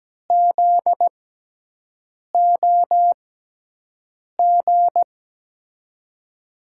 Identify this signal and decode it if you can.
Morse code, heard as a keyed tone, about 17 wpm, spaced as for 6 wpm, 700 Hz. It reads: ZOG
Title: USA Call Signs - 5 Characters - 6x Character Spacing - 17wpm